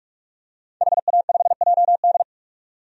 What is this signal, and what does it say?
Morse code, heard as a keyed tone, about 42 wpm, 700 Hz. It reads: HA5JD
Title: DXCC-100 Call Signs - 5 Characters - 45wpm